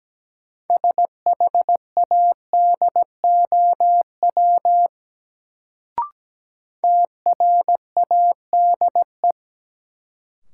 Morse code, heard as a keyed tone, about 17 wpm, 700 Hz. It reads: SHADOW TRADE
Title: Sets of 5 Words Encore - New Word Drill - 17wpm